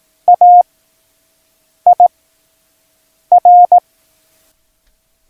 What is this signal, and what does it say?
Morse code, heard as a keyed tone, about 18 wpm, 710 Hz. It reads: A I R